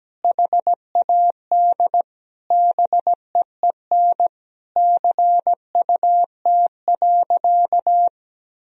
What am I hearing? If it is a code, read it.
Morse code, heard as a keyed tone, about 17 wpm, 700 Hz. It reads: HAD BEEN CUT.
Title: Sentences from Top 1000 Words - Review Part 1 - 17wpm